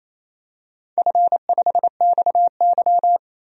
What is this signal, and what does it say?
Morse code, heard as a keyed tone, about 27 wpm, 700 Hz. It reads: F5XY
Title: DXCC-100 Call Signs - 4 Characters - 28wpm